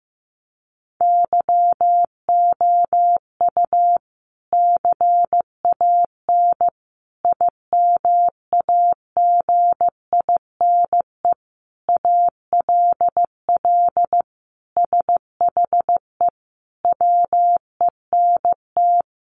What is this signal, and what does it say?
Morse code, heard as a keyed tone, about 15 wpm, 700 Hz. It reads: YOU CAN IMAGINE ALL SHE WENT
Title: Sentences from Top 1400 Words 15wpm